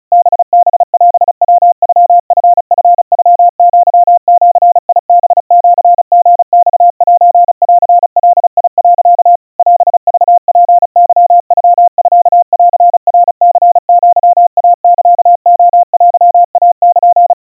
Morse code (Japanese Wabun code), heard as a keyed tone, about 35 wpm, 700 Hz. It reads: ハハオヤノチチノアジハシリマセンガ、 オクツケノミンナニアイサレテイル
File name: kawa / Morse sample / kamyu 130 cpm 700 Hz Crispy